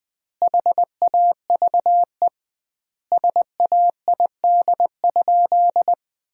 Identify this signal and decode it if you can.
Morse code, heard as a keyed tone, about 21 wpm, 700 Hz. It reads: HAVE SAID?
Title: 2x Word-Spacing - Sentences from Top 100 Words - 20wpm